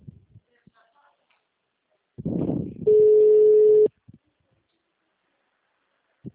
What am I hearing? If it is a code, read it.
Ringback tone (Europe, 425 Hz).
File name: out- -2005- - - .83133